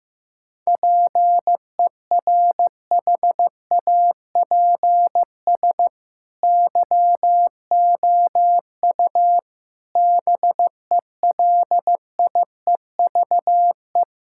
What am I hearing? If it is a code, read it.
Morse code, heard as a keyed tone, about 15 wpm, 700 Hz. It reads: PERHAPS YOU BELIEVE
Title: Sentences from Top 1600 Words 15wpm